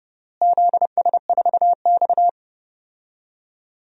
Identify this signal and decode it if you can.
Morse code, heard as a keyed tone, about 30 wpm, 700 Hz. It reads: ZS4X
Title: DXCC-100 Call Signs - 4 Characters - Repeated 3x - 30wpm